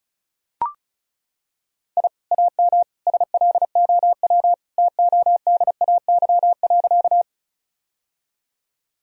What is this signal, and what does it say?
Morse code, heard as a keyed tone, about 36 wpm, 700 Hz. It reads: I AM SLOW TODAY.
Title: Sentences from Top 1000 Words - Review Part 2 - 35wpm